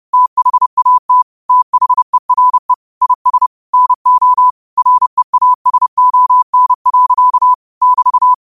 Morse code, heard as a keyed tone, about 30 wpm, 1000 Hz. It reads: THAT THERE IS NO REASON. <BT>
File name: qotdfiles / 30wpm